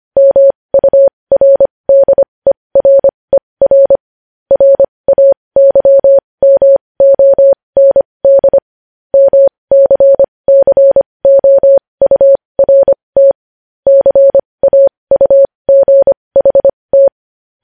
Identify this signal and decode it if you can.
Morse code, heard as a keyed tone, about 25 wpm, spaced as for 20 wpm, 550 Hz. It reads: MURDERER RAYMOND MCCOURT CAUGHT
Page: news headlines in morse code at 20 wpm